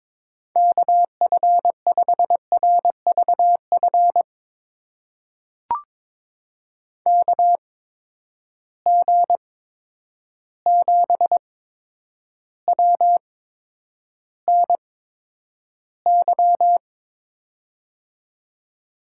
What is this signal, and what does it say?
Morse code, heard as a keyed tone, about 22 wpm, spaced as for 9 wpm, 700 Hz. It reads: KF5RVF KG7WNY
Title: USA Call Signs - 6 Characters - 8x Character Spacing - 22wpm